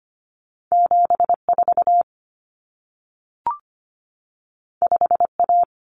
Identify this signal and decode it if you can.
Morse code, heard as a keyed tone, about 25 wpm, 700 Hz. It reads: 74 5A
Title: Common Cut Numbers - 2-Digit 25wpm